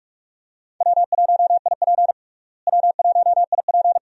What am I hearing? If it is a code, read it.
Morse code, heard as a keyed tone, about 42 wpm, 700 Hz. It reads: W1IP W1IP
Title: USA Call Signs - 4 Characters - Repeated 2x - 45wpm